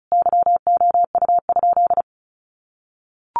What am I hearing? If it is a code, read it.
Morse code, heard as a keyed tone, about 34 wpm, 700 Hz. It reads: YOU?
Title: Sentences from Top 200 Words 35wpm